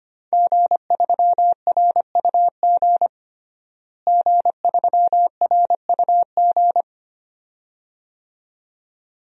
Morse code, heard as a keyed tone, about 25 wpm, 700 Hz. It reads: G3RUG G3RUG
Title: DXCC-100 Call Signs - 5 Characters - Repeated 2x - 25wpm